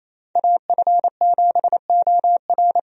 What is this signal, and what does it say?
Morse code, heard as a keyed tone, about 27 wpm, 700 Hz. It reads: AF7OR
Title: USA Call Signs - 5 Characters - 28wpm